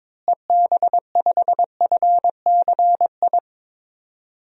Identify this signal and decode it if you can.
Morse code, heard as a keyed tone, about 22 wpm, 700 Hz. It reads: EB5FCI
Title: DXCC-100 Call Signs - 6 Characters - Repeated 3x - 22wpm